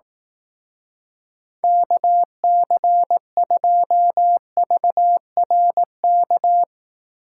Morse code, heard as a keyed tone, about 18 wpm, 700 Hz. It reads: KC2VRK
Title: Speed Racing - USA 2x3 Call Signs 15wpm